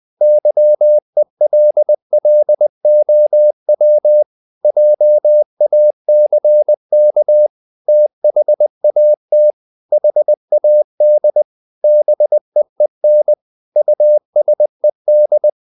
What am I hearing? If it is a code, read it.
Morse code, heard as a keyed tone, about 20 wpm, 600 Hz. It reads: YELLOW JACK THAT HAD BEEN USED
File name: the old man and the sea / the old man and the sea part 1 20wpm